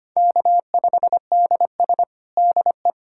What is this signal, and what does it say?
Morse code, heard as a keyed tone, about 25 wpm, 700 Hz. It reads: K5DS DE
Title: USA Callsign DE Callsign - 4 Characters - 25wpm